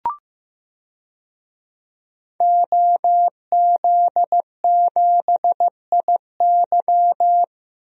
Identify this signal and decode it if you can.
Morse code, heard as a keyed tone, about 15 wpm, 700 Hz. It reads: OZ7IY
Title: DXCC-100 Call Signs - 5 Characters - Repeated 3x - 15wpm